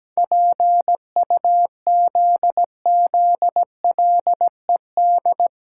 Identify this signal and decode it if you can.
Morse code, heard as a keyed tone, about 17 wpm, 700 Hz. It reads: PUZZLED
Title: Pangram Challenge v2 - 17wpm